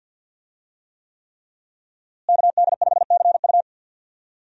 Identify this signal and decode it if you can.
Morse code, heard as a keyed tone, about 52 wpm, 700 Hz. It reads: KD5CU
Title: USA Call Signs - 5 Characters - Repeated 3x - 50wpm